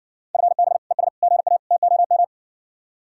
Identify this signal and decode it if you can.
Morse code, heard as a keyed tone, about 60 wpm, 700 Hz. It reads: FB ES GA TOM